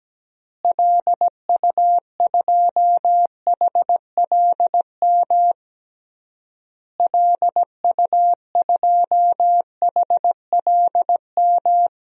Morse code, heard as a keyed tone, about 17 wpm, 700 Hz. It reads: LU2HLM LU2HLM
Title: DXCC-100 Call Signs - 6 Characters - Repeated 3x - 17wpm